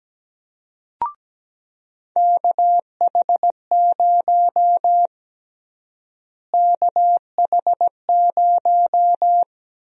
Morse code, heard as a keyed tone, about 17 wpm, 700 Hz. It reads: KH0 KH0
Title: USA Call Sign Prefixes - Repeated 2x - 17wpm